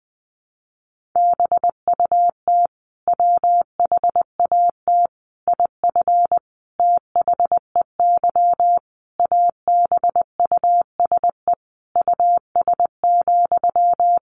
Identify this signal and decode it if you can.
Morse code, heard as a keyed tone, about 20 wpm, 700 Hz. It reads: BUT WHAT IF THEY ABUSE US,
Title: Sentences from Top 1400 Words 20wpm